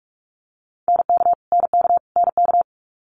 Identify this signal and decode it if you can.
Morse code, heard as a keyed tone, about 46 wpm, 700 Hz. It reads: NK NK NK
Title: Multi-letter Phonemes Repeated 3x 45wpm